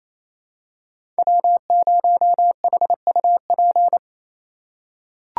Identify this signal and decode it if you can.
Morse code, heard as a keyed tone, about 28 wpm, 700 Hz. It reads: W0HUP
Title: USA Call Signs - 3x Character Spacing - 28wpm